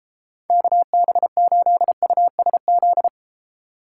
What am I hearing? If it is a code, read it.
Morse code, heard as a keyed tone, about 30 wpm, 700 Hz. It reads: KB8USZ